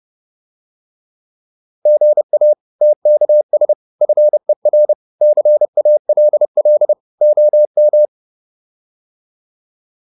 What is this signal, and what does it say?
Morse code, heard as a keyed tone, about 30 wpm, 600 Hz. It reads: GA TKS FER CALL OM